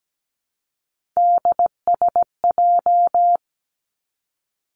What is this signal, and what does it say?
Morse code, heard as a keyed tone, about 17 wpm, 700 Hz. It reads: DSJ
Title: Three Challenging Letter-Number Combinations 17wpm